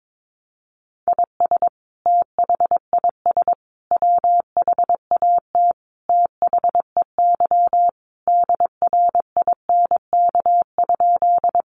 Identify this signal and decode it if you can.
Morse code, heard as a keyed tone, about 22 wpm, 700 Hz. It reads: IS THIS WHAT THEY DRINK?
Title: Sentences from Top 1500 Words 22wpm